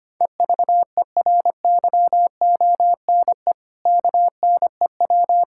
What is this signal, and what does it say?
Morse code, heard as a keyed tone, about 25 wpm, 700 Hz. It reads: EVERYONE KNEW
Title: Sentences from Top 1000 Words - Review Part 5 - 25wpm